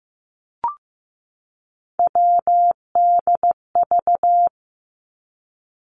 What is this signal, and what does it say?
Morse code, heard as a keyed tone, about 15 wpm, 700 Hz. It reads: WDV
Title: Three Challenging Letter-Number Combinations 15wpm